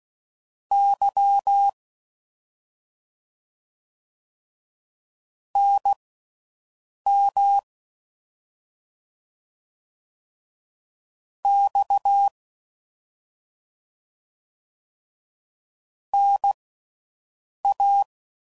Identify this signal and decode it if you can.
Morse code, heard as a keyed tone, about 16 wpm, spaced as for 6 wpm, 800 Hz. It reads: Y NM X NA
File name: Mixed / CW Comparisons / Comparisons Broken Characters